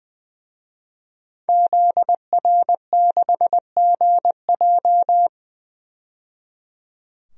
Morse code, heard as a keyed tone, about 20 wpm, 700 Hz. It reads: ZR6GJ
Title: DXCC-100 Call Signs - 5 Characters - 20wpm